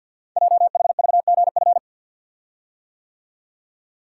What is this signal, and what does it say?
Morse code, heard as a keyed tone, about 50 wpm, 700 Hz. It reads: JHVGF
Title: Code Groups - 5 Letters - 50wpm